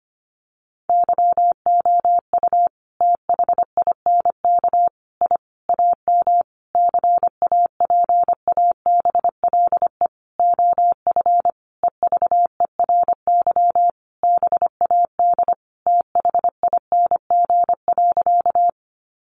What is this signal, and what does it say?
Morse code, heard as a keyed tone, about 25 wpm, 700 Hz. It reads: YOU THINK I AM CAPABLE OF EVERY BAD THING.
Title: Sentences from Top 1200 Words 25wpm